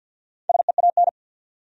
Morse code, heard as a keyed tone, about 51 wpm, 700 Hz. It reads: SEAN
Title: USA Common Names - Max 4 Letters - 50wpm